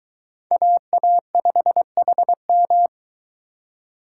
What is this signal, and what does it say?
Morse code, heard as a keyed tone, about 22 wpm, 700 Hz. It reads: AA5HM